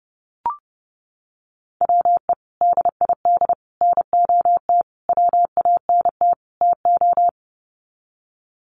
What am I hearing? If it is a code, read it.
Morse code, heard as a keyed tone, about 30 wpm, 700 Hz. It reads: WE DID NOT WANT TO
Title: Sets of 5 Words Encore 30wpm